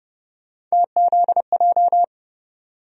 Morse code, heard as a keyed tone, about 29 wpm, 700 Hz. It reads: TZJ
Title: USA Call Sign Postfixes - 3 Letters 30wpm